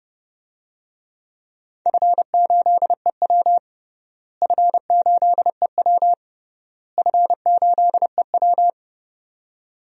Morse code, heard as a keyed tone, about 30 wpm, 700 Hz. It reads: F8EW F8EW F8EW
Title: DXCC-100 Call Signs - 4 Characters - Repeated 3x - 30wpm